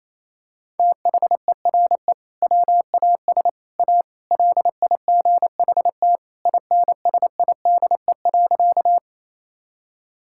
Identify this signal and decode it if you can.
Morse code, heard as a keyed tone, about 28 wpm, 700 Hz. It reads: THERE WAS A LIGHT INSIDE.
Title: Sentences from Top 1000 Words - Review Part 6 - 28wpm